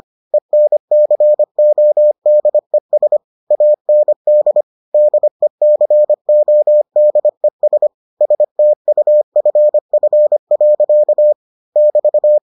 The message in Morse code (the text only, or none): ENCODES AND DECODES STUFF. <BT>